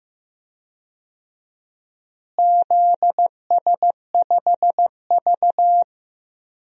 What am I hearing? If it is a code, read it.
Morse code, heard as a keyed tone, about 15 wpm, 700 Hz. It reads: ZS5V
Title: DXCC-100 Call Signs - 4 Characters - Repeated 2x - 15wpm